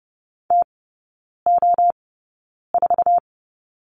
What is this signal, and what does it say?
Morse code, heard as a keyed tone, about 30 wpm, spaced as for 9 wpm, 700 Hz. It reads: TO4
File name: course / 30:10wpm / 087 Lesson 15e - Number 5 - Three Characters 30wpm